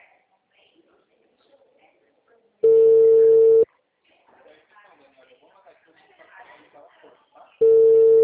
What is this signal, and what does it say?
Ringback tone (China, 450 Hz), with speech.
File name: out- -2019- - - .53626